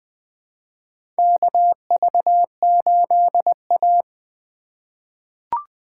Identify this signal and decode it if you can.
Morse code, heard as a keyed tone, about 21 wpm, 700 Hz. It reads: KV8A
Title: USA Call Signs - 4 Characters - 8x Character Spacing - 20wpm